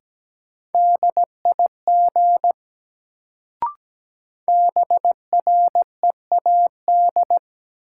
Morse code, heard as a keyed tone, about 17 wpm, 700 Hz. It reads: DIG BREAD
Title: Sentences from Top 1600 Words - New Word Drill - 17wpm